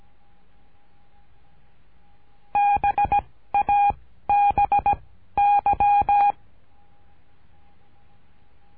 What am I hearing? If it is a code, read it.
Morse code, heard as a keyed tone, about 17 wpm, 810 Hz. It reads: B A B Y
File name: cursos / CURSO DE TELEGRAFIA / Telegrafia - Trilha 13